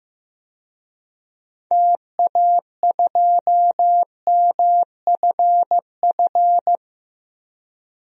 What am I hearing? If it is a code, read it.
Morse code, heard as a keyed tone, about 15 wpm, 700 Hz. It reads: TA2MFF
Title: DXCC-100 Call Signs - 6 Characters - 15wpm